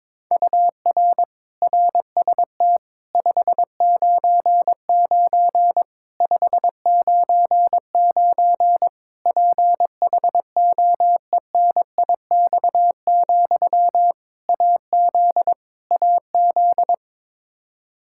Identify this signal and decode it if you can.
Morse code, heard as a keyed tone, about 22 wpm, 700 Hz. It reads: UR RST 599 599 PHOENIX, AZ AZ
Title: K1USN SST QSO Practice 22wpm